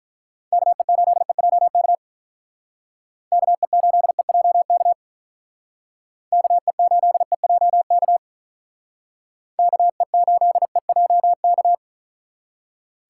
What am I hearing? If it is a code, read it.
Morse code, heard as a keyed tone, about 43 wpm, 700 Hz. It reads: KE8EJK KE8EJK KE8EJK KE8EJK
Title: Speed Racing - USA 2x3 Call Signs 35wpm